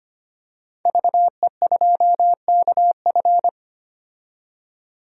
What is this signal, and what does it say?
Morse code, heard as a keyed tone, about 25 wpm, 690 Hz. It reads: VE2KF
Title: DXCC-100 Call Signs - 5 Characters - 25wpm